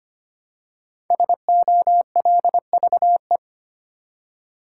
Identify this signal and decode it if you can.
Morse code, heard as a keyed tone, about 25 wpm, 700 Hz. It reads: SOLVE
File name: 5 Letter Words - First 3 Letters Given - 25wpm